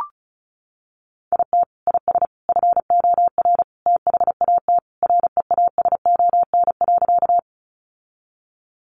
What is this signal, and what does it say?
Morse code, heard as a keyed tone, about 34 wpm, 700 Hz. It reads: IT IS FOR THAT REASON.
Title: Sentences from Top 700 Words 35wpm